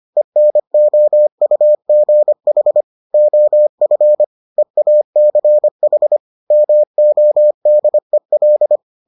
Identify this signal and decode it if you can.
Morse code, heard as a keyed tone, about 25 wpm, 600 Hz. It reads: ENOUGH OF EACH MODEL